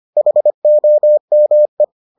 Morse code, heard as a keyed tone, about 25 wpm, 600 Hz. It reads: HOME